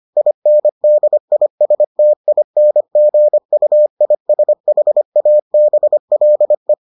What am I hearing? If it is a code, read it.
Morse code, heard as a keyed tone, about 25 wpm, 600 Hz. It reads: INDISTINGUISHABLE